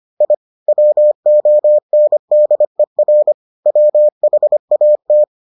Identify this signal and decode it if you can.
Morse code, heard as a keyed tone, about 25 wpm, 600 Hz. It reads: I WONDER WHAT